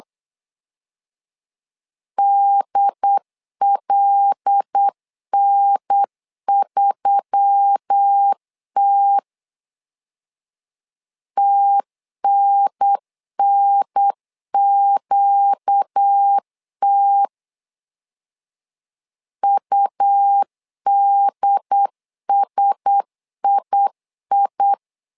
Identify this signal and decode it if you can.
Morse code, heard as a keyed tone, about 8 wpm, 800 Hz. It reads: DLN3T TNNQT UDSII
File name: LEZIONI / 5 LBVC / 5 L 8wpm 2831